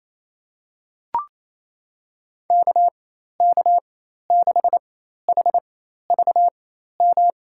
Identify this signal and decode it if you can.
Morse code, heard as a keyed tone, about 28 wpm, spaced as for 13 wpm, 700 Hz. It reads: KK6HVM
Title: USA Call Signs - 6 Characters - 4x Character Spacing - 28wpm